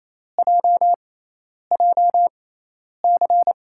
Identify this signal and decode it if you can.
Morse code, heard as a keyed tone, about 28 wpm, spaced as for 10 wpm, 700 Hz. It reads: JJC
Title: USA Call Signs - 6x Character Spacing - 28wpm